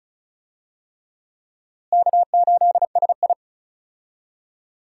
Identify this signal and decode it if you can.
Morse code, heard as a keyed tone, about 36 wpm, 700 Hz. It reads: K8SI
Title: USA Call Signs - 4 Characters - Repeated 3x - 35wpm